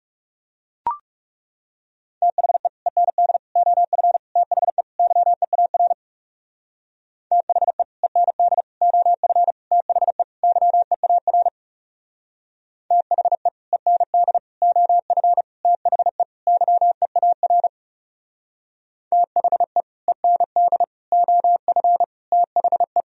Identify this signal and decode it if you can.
Morse code, heard as a keyed tone, about 36 wpm, 700 Hz. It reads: THE END OF THE YEAR THE END OF THE YEAR THE END OF THE YEAR THE END OF THE
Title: Speed-Racing - Sets of 5 Words 30wpm